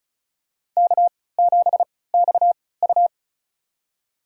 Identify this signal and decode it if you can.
Morse code, heard as a keyed tone, about 34 wpm, spaced as for 20 wpm, 700 Hz. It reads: K7XU
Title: USA Call Signs - 3x Character Spacing - 35wpm